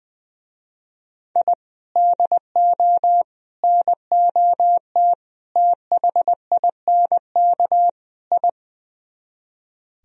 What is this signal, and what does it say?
Morse code, heard as a keyed tone, about 22 wpm, 700 Hz. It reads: I DO NOT THINK I
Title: Speed-Racing - Sets of 5 Words 20wpm